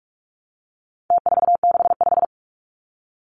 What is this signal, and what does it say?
Morse code, heard as a keyed tone, about 44 wpm, 700 Hz. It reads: T465
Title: Numbers - Contest Style - 4-Digit 45wpm